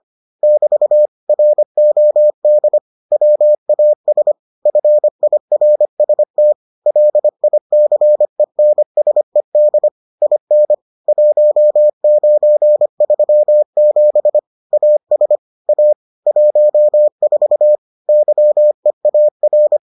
Morse code, heard as a keyed tone, about 25 wpm, 600 Hz. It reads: <BT> ROD WAS FIRST LICENSED IN 1937 AS A 14 YEAR